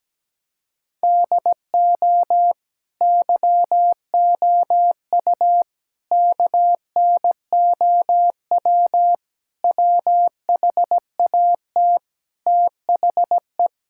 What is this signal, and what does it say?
Morse code, heard as a keyed tone, about 17 wpm, 700 Hz. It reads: DO YOU KNOW WHAT THE
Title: Sentences from Top 1600 Words 17wpm